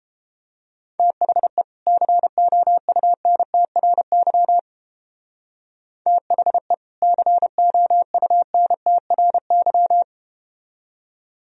Morse code, heard as a keyed tone, about 32 wpm, 700 Hz. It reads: THE COUNTRY THE COUNTRY